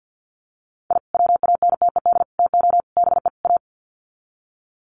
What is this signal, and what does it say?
Morse code, heard as a keyed tone, about 50 wpm, 700 Hz. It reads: I WANTED TO BE A
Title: Sets of 5 Words Encore2 50wpm